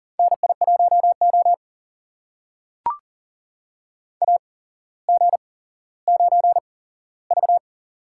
Morse code, heard as a keyed tone, about 40 wpm, spaced as for 17 wpm, 700 Hz. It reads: NI1O AG9V
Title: USA Call Signs - 4 Characters - 8x Character Spacing - 40wpm